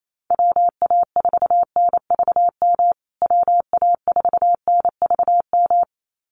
Morse code, heard as a keyed tone, about 28 wpm, 700 Hz. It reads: WA4NVM WA4NVM
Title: USA Call Signs Repeated 2x 28wpm